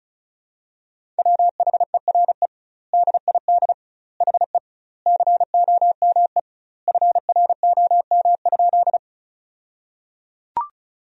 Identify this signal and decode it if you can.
Morse code, heard as a keyed tone, about 36 wpm, 700 Hz. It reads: WHERE DID HE COME FROM?
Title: 2x Word-Spacing - Sentences from Top 200 Words - 35wpm